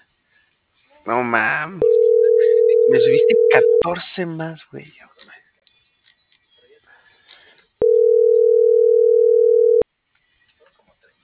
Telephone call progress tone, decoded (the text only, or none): ringback tone